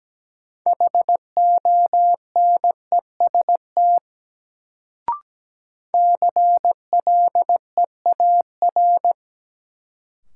Morse code, heard as a keyed tone, about 17 wpm, 700 Hz. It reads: HONEST CLEAR
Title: Sets of 5 Words Encore - New Word Drill - 17wpm